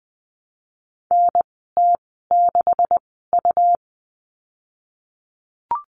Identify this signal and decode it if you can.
Morse code, heard as a keyed tone, about 20 wpm, spaced as for 15 wpm, 700 Hz. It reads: NT6U
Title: 140 Lesson 23g - Letter f - Call Signs 20wpm